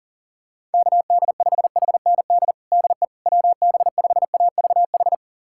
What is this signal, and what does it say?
Morse code, heard as a keyed tone, about 40 wpm, 690 Hz. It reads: KD5HND DE WB5AVH
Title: USA Callsign DE Callsign - 6 Characters - 40wpm